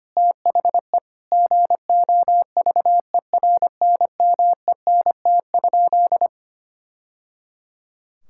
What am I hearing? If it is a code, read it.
Morse code, heard as a keyed tone, about 25 wpm, 700 Hz. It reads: THE GOVERNMENT?
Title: Sentences from Top 1000 Words - Review Part 5 - 25wpm